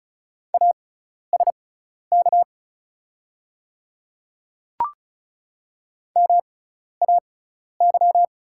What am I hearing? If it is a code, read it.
Morse code, heard as a keyed tone, about 36 wpm, spaced as for 12 wpm, 700 Hz. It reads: ASK MAY
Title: 6x Character Spacing - Top 200 Words - 35wpm